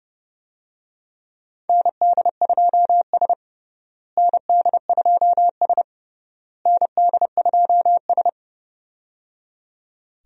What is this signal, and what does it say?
Morse code, heard as a keyed tone, about 30 wpm, 700 Hz. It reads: ND2S ND2S ND2S
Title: USA Call Signs - 4 Characters - Repeated 3x - 30wpm